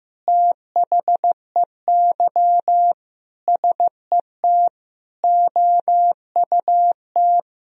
Morse code, heard as a keyed tone, about 16 wpm, 700 Hz. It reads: THEY SET OUT
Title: Sentences from Top 1000 Words - Review Part 1 - 15wpm